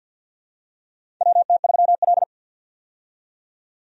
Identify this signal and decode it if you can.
Morse code, heard as a keyed tone, about 53 wpm, 700 Hz. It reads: WT3L
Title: USA Call Signs - 4 Characters - 50wpm